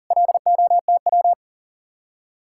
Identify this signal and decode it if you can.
Morse code, heard as a keyed tone, about 48 wpm, 700 Hz. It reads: LOTW
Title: Speed-Racing Common QSO Phrases Encore - 40wpm